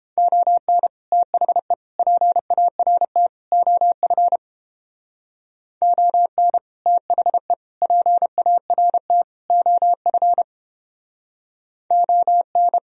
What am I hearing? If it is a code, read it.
Morse code, heard as a keyed tone, about 29 wpm, 700 Hz. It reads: ON THE PART OF ON THE PART OF ON